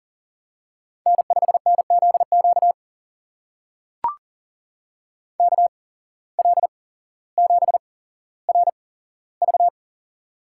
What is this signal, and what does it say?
Morse code, heard as a keyed tone, about 39 wpm, spaced as for 17 wpm, 700 Hz. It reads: N5NZQ KL7RV
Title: USA Call Signs - 5 Characters - 8x Character Spacing - 40wpm